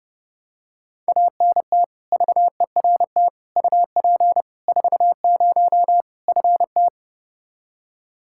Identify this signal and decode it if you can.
Morse code, heard as a keyed tone, about 30 wpm, 700 Hz. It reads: ANT VERT UP 40 FT
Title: Practice QSO - Exchange 2 - Info - 30wpm